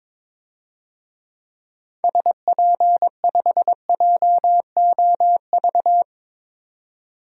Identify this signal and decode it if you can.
Morse code, heard as a keyed tone, about 21 wpm, 700 Hz. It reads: SP5JOV
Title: DXCC-100 Call Signs - 6 Characters - 22wpm